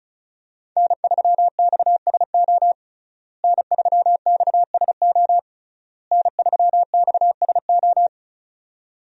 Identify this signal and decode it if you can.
Morse code, heard as a keyed tone, about 36 wpm, 700 Hz. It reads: N3XSO N3XSO N3XSO
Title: USA Call Signs - 5 Characters - Repeated 3x - 35wpm